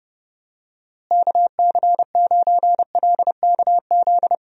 Morse code, heard as a keyed tone, about 32 wpm, 700 Hz. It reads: KC9LKZ
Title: USA Call Signs - 6 Characters - Repeated 2x - 30wpm